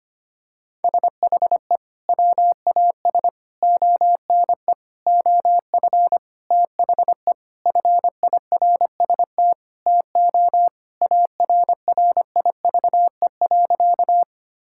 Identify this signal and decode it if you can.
Morse code, heard as a keyed tone, about 25 wpm, 700 Hz. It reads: SHE WAS ONE OF THE FIRST TO ARRIVE.
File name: practice / 25wpm / Sentences with 4-Word Phrases - 25wpm